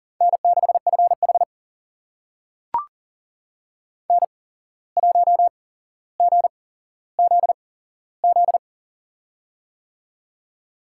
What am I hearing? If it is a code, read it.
Morse code, heard as a keyed tone, about 40 wpm, spaced as for 16 wpm, 700 Hz. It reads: N6FH N1GZZ
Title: USA Call Signs - 8x Character Spacing - 40wpm